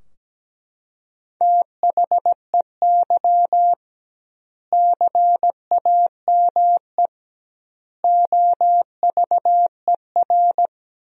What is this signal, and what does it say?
Morse code, heard as a keyed tone, about 17 wpm, 700 Hz. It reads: THEY CAME OVER
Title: 2x Word-Spacing - Sentences from Top 300 Words - 17wpm